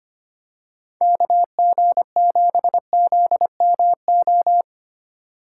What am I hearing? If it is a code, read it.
Morse code, heard as a keyed tone, about 25 wpm, 700 Hz. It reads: KG7ZMO
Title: USA Call Signs - 6 Characters - 2x Character Spacing - 25wpm